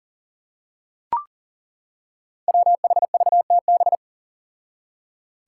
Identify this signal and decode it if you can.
Morse code, heard as a keyed tone, about 40 wpm, 700 Hz. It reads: WHVTB
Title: Five Challenging Letter-Number Combinations 40wpm